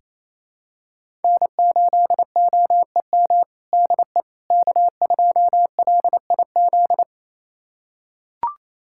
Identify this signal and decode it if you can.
Morse code, heard as a keyed tone, about 27 wpm, 700 Hz. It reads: N8OEM DE K2LIZ
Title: USA Callsign DE Callsign 28wpm